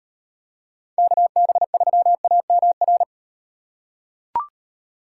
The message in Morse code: KB3AMR

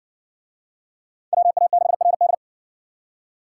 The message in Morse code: WA6RD

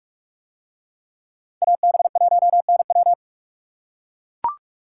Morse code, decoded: AB1NW

